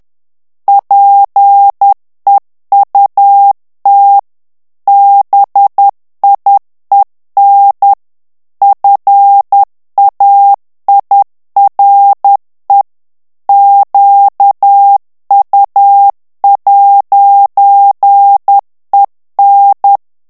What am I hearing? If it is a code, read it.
Morse code, heard as a keyed tone, about 10 wpm, 800 Hz. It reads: PEUT BIEN FAIRE QU'EN
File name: FRANCAIS TEXTE CLAIR 2 VITESSE 10 MOTS MINUTES